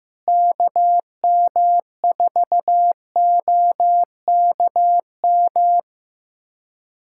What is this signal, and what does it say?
Morse code, heard as a keyed tone, about 15 wpm, 700 Hz. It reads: KM4OKM
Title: USA Callsign DE Callsign 15wpm